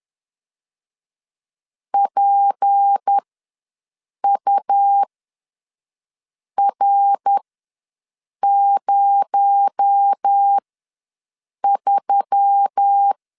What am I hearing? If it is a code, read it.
Morse code, heard as a keyed tone, about 10 wpm, spaced as for 5 wpm, 800 Hz. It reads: PUR03